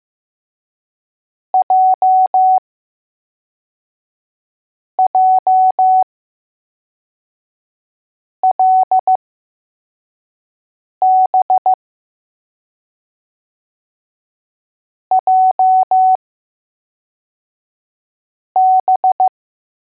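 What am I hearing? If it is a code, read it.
Morse code, heard as a keyed tone, about 15 wpm, spaced as for 4 wpm, 750 Hz. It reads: JJLB JB
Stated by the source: Morse code letters test drill for L, J and B.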